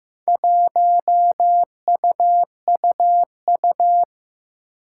Morse code, heard as a keyed tone, about 15 wpm, 700 Hz. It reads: 1UUU
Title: Four Challenging Letter-Number Combinations 15wpm